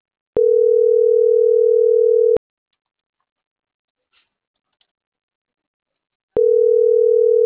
Ringback tone (North America), 440 + 480 Hz.